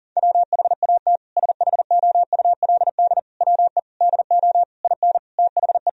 Morse code, heard as a keyed tone, about 40 wpm, 700 Hz. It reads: WHAT SHOULD WE DO IN THE
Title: Sentences from Top 1000 Words - Review Part 4 - 40wpm